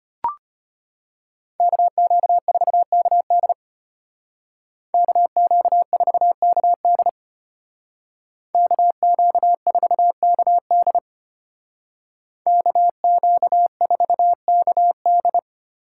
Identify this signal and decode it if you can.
Morse code, heard as a keyed tone, about 33 wpm, 700 Hz. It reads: KQ4KD KQ4KD KQ4KD KQ4KD